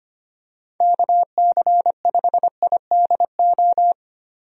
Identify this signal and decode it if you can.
Morse code, heard as a keyed tone, about 25 wpm, 700 Hz. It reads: KC5IDO
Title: USA Call Signs - 6 Characters - 25wpm